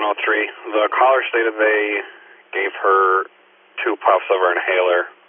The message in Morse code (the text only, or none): none